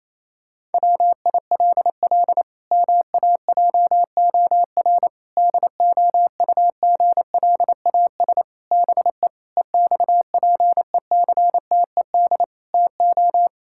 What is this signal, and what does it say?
Morse code, heard as a keyed tone, about 28 wpm, 700 Hz. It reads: WILL MAJOR DOUGLAS BE EXPECTED TO
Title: Pangram Challenge v2 - 28wpm